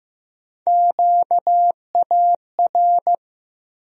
Morse code, heard as a keyed tone, about 15 wpm, 700 Hz. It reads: QAR
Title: USA Call Sign Postfixes - 3 Letters 15wpm